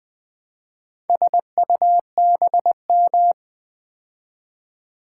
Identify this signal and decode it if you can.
Morse code, heard as a keyed tone, about 20 wpm, 700 Hz. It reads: SUBM
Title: Build A Word - 2 Letters At A Time - 6-Letter Words - 20wpm